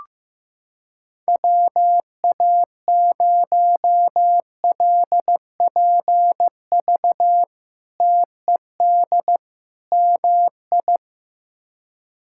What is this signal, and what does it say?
Morse code, heard as a keyed tone, about 16 wpm, 700 Hz. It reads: WA0LPV TED MI
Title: K1USN SST QSO Practice - 15wpm